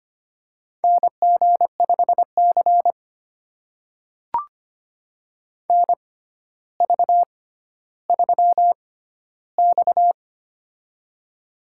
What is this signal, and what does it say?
Morse code, heard as a keyed tone, about 25 wpm, spaced as for 13 wpm, 700 Hz. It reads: NG5C NV3X
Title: USA Call Signs - 4 Characters - 6x Character Spacing - 25wpm